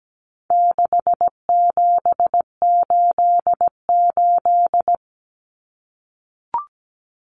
Morse code, heard as a keyed tone, about 17 wpm, 700 Hz. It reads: 6788
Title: Numbers - 4-Digit 17wpm